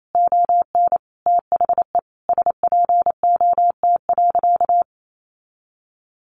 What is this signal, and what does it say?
Morse code, heard as a keyed tone, about 28 wpm, 700 Hz. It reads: ON THE SPOT.
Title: Sentences from Top 1200 Words 28wpm